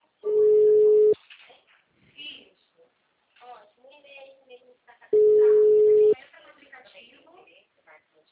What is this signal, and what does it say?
Ringback tone (Europe, 425 Hz), with speech.